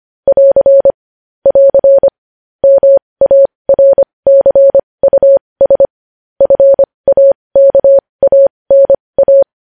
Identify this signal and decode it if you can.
Morse code, heard as a keyed tone, about 25 wpm, spaced as for 20 wpm, 550 Hz. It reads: <AR> <AR> MARCUS FAKANA
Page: news headlines in morse code at 20 wpm